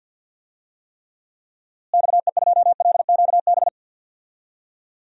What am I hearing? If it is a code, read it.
Morse code, heard as a keyed tone, about 52 wpm, 700 Hz. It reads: XE2LQB
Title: DXCC-100 Call Signs - 6 Characters - Repeated 3x - 50wpm